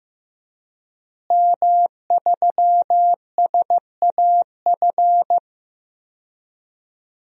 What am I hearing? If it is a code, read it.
Morse code, heard as a keyed tone, about 15 wpm, 700 Hz. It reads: M3SAF
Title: DXCC-100 Call Signs - 5 Characters - Repeated 3x - 15wpm